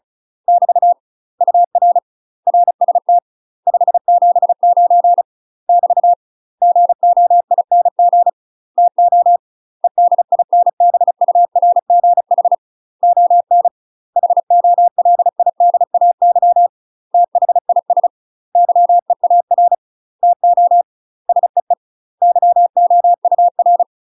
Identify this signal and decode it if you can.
Morse code, heard as a keyed tone, about 36 wpm, 700 Hz. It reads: <BT> UR RST 579 <BT> GOING TO EDINBURGH ON HOLIDAY THIS YEAR TO SEE YOUR
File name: UK MASTER QSOs TRAINING/35WPM